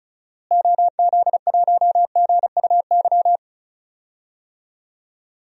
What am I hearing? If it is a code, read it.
Morse code, heard as a keyed tone, about 36 wpm, 700 Hz. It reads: OZ1GUY
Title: DXCC-100 Call Signs - 6 Characters - Repeated 3x - 35wpm